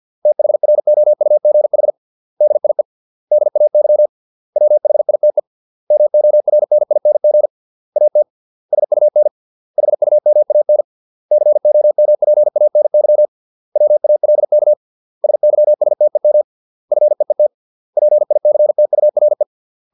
Morse code, heard as a keyed tone, about 50 wpm, 600 Hz. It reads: THROUGH DIE DAY WHITE MORNING AT SUN HUMAN COMPANY WALK SYSTEM FEET PICTURE